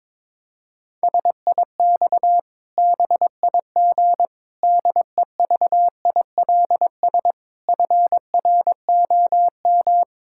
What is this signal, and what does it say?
Morse code, heard as a keyed tone, about 22 wpm, 700 Hz. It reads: SIX BIG DEVILS FROM
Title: Pangram Challenge v2 - 22wpm